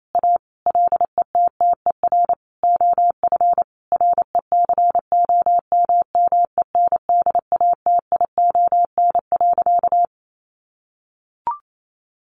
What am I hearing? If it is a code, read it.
Morse code, heard as a keyed tone, about 28 wpm, 700 Hz. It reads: A LETTER OF RECOMMENDATION.
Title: Sentences from Top 1400 Words 28wpm